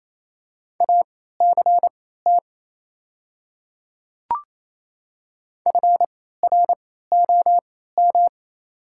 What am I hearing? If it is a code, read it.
Morse code, heard as a keyed tone, about 28 wpm, spaced as for 16 wpm, 690 Hz. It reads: ACT FROM
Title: 3x Character Spacing - Top 200 Words - 28wpm